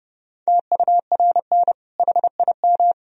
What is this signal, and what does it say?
Morse code, heard as a keyed tone, about 32 wpm, 700 Hz. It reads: TURN HIM
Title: Sentences from Top 1000 Words - Review Part 1 - 30wpm